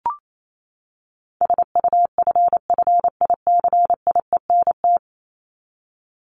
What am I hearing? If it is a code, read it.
Morse code, heard as a keyed tone, about 28 wpm, 700 Hz. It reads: SUFFICIENT
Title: Top 1251-1500 Words 28wpm